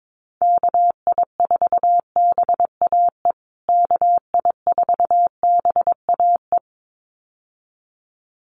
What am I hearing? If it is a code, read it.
Morse code, heard as a keyed tone, about 22 wpm, 700 Hz. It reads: KI4BAE KI4BAE
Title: USA Call Signs Repeated 2x 22wpm